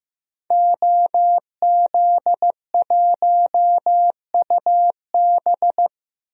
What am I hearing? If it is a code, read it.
Morse code, heard as a keyed tone, about 15 wpm, 700 Hz. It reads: OZ1UB
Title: DXCC-100 Call Signs - 5 Characters - Repeated 2x - 15wpm